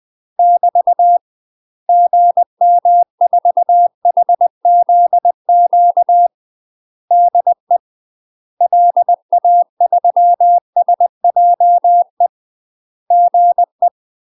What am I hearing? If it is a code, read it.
Morse code, heard as a keyed tone, about 20 wpm, 700 Hz. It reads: <BT> GM4HZQ DE LA3SJE GE
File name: UK MASTER QSOs TRAINING / 20WPM / UK MASTER QSOs 0001